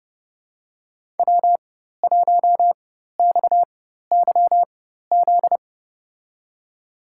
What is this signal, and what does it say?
Morse code, heard as a keyed tone, about 30 wpm, spaced as for 14 wpm, 700 Hz. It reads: W1XYZ
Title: USA Call Signs - 5 Characters - 4x Character Spacing - 30wpm